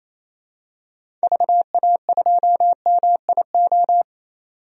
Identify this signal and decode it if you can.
Morse code, heard as a keyed tone, about 28 wpm, 700 Hz. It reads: VA2MIO
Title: DXCC-100 Call Signs - 6 Characters - 28wpm